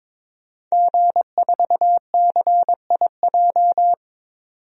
Morse code, heard as a keyed tone, about 22 wpm, 700 Hz. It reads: G4CIJ
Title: DXCC-100 Call Signs - 5 Characters - 22wpm